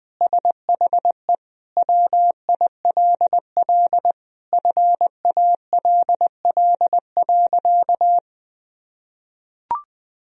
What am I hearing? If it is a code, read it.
Morse code, heard as a keyed tone, about 20 wpm, 700 Hz. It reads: SHE WILL FALL.
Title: Sentences from Top 1000 Words - Review Part 2 - 20wpm